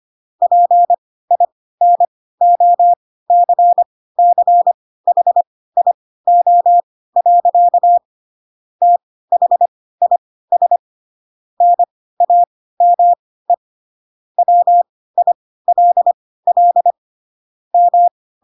Morse code, heard as a keyed tone, about 25 wpm, spaced as for 16 wpm, 700 Hz. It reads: PINOCCHIO. THIS NAME WILL M
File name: Pinocchio / Pinocchio ch3 10 25 ik1hge